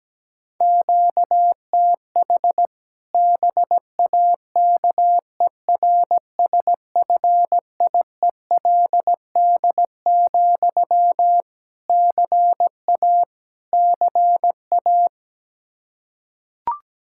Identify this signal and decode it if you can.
Morse code, heard as a keyed tone, about 17 wpm, 700 Hz. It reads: QTH BAKERSFIELD, CA CA
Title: Practice QSO - Exchange 1 - Beginning - 17wpm